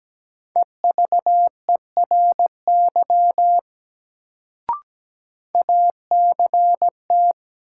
Morse code, heard as a keyed tone, about 17 wpm, 700 Hz. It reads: EVERY ACT
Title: Sentences from Top 200 Words - New Word Drill - 17wpm